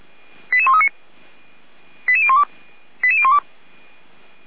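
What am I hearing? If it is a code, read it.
Five-tone selective call (ZVEI), heard as quick runs of five tones, tones of about 70 ms. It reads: 88139 88012 88012
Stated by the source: ZVEI five-tone selective call received on an SDR, on the audio server of Feuerwehr Orlamünde.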